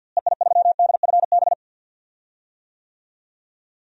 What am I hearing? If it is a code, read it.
Morse code, heard as a keyed tone, about 51 wpm, 700 Hz. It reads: EI3DFB
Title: DXCC-100 Call Signs - 6 Characters - Repeated 3x - 50wpm